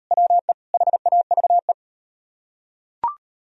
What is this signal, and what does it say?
Morse code, heard as a keyed tone, about 37 wpm, 700 Hz. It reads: WE HAVE